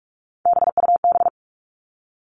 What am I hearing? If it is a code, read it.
Morse code, heard as a keyed tone, about 44 wpm, 700 Hz. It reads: BUB
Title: All Cut Numbers - 3-Digit 45wpm